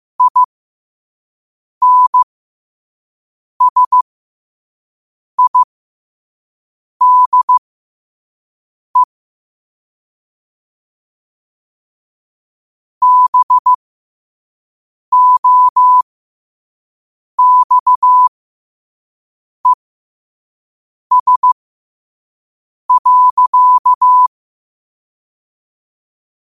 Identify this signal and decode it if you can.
Morse code, heard as a keyed tone, about 15 wpm, spaced as for 5 wpm, 1000 Hz. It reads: INSIDE BOXES.
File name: qotdfiles / 5wpm